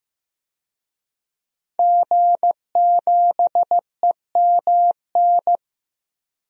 Morse code, heard as a keyed tone, about 15 wpm, 700 Hz. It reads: G7EMN
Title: DXCC-100 Call Signs - 5 Characters - Repeated 3x - 15wpm